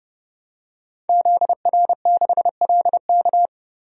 Morse code, heard as a keyed tone, about 30 wpm, 700 Hz. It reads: ZR6LK
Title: DXCC-100 Call Signs - 5 Characters - 30wpm